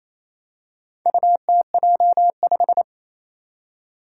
Morse code, heard as a keyed tone, about 28 wpm, 700 Hz. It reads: UTJ5
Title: Four Challenging Letter-Number Combinations 28wpm